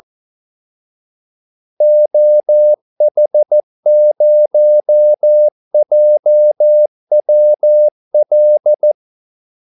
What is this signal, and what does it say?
Morse code, heard as a keyed tone, about 14 wpm, 600 Hz. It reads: OH0JWL
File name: Callsigns-14wpm